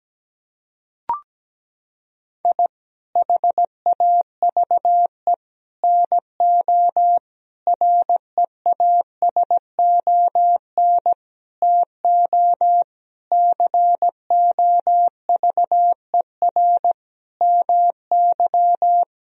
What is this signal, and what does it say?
Morse code, heard as a keyed tone, about 17 wpm, 700 Hz. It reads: I HAVE NO REASON TO COVER MY
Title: Sentences from Top 1000 Words - Review Part 1 - 17wpm